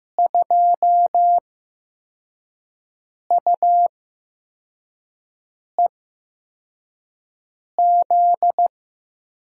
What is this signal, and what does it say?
Morse code, heard as a keyed tone, about 15 wpm, spaced as for 4 wpm, 700 Hz. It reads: 2UEZ
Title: USA Call Signs - 8x Character Spacing - 15wpm